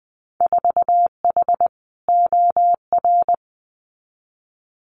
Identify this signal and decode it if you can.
Morse code, heard as a keyed tone, about 20 wpm, 700 Hz. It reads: 4H OR
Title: Winter Field Day Exchanges Encore 20wpm